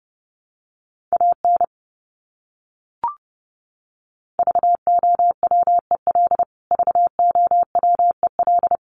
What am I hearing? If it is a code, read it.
Morse code, heard as a keyed tone, about 30 wpm, 700 Hz. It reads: AN VOWEL VOWEL
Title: Top 500 Words Repeated 30wpm